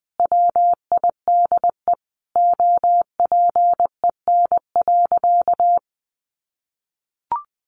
Morse code, heard as a keyed tone, about 20 wpm, 700 Hz. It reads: WIDE OPEN.
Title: Sentences from Top 1500 Words 20wpm